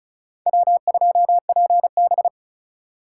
Morse code, heard as a keyed tone, about 46 wpm, 700 Hz. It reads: W2PB